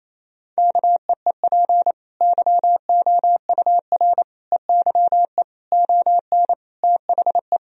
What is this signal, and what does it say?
Morse code, heard as a keyed tone, about 28 wpm, 700 Hz. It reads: KEEP YOUR EYE ON THE
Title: Sentences from Top 1000 Words - Review Part 4 - 28wpm